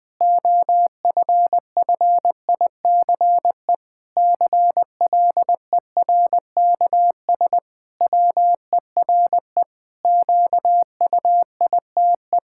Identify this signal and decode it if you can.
Morse code, heard as a keyed tone, about 20 wpm, 700 Hz. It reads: OFFICE CLERKS WERE QUITE
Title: Pangram Challenge v2 - 20wpm